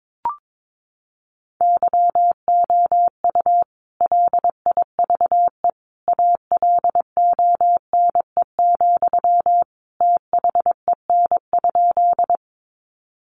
Morse code, heard as a keyed tone, about 22 wpm, 700 Hz. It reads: YOU LIVE ALONE, THEN?
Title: Sentences from Top 1000 Words 22wpm